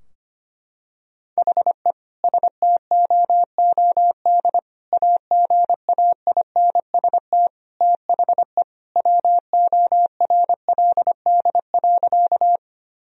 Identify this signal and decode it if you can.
Morse code, heard as a keyed tone, about 25 wpm, 700 Hz. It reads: HE STOOD AGAINST THE WORLD.
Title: Sentences from Top 1000 Words - Review Part 2 - 25wpm